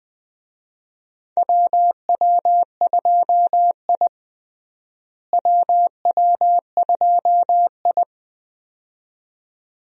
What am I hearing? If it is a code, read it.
Morse code, heard as a keyed tone, about 21 wpm, 700 Hz. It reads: WW2I WW2I
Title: USA Call Signs - 4 Characters - Repeated 2x - 20wpm